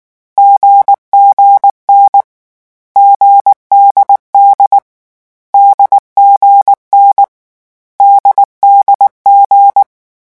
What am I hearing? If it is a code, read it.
Morse code, heard as a keyed tone, about 19 wpm, 800 Hz. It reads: GGN GDD DGN DDG
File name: morse mp3 / lecon4-vitesse 18